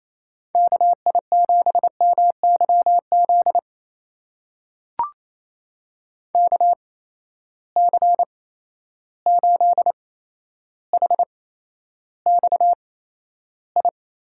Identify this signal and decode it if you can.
Morse code, heard as a keyed tone, about 28 wpm, spaced as for 12 wpm, 700 Hz. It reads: KI7MYZ KC8HXI
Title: USA Call Signs - 6 Characters - 8x Character Spacing - 28wpm